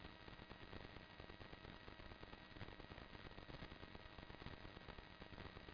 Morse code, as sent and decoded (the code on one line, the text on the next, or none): none